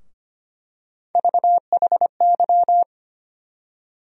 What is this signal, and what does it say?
Morse code, heard as a keyed tone, about 25 wpm, 700 Hz. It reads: VHY
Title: USA Call Sign Postfixes - 3 Letters 25wpm